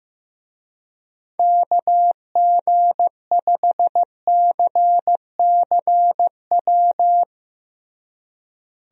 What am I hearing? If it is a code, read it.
Morse code, heard as a keyed tone, about 15 wpm, 700 Hz. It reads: KG5CCW
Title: USA Call Signs - 6 Characters - Repeated 3x - 15wpm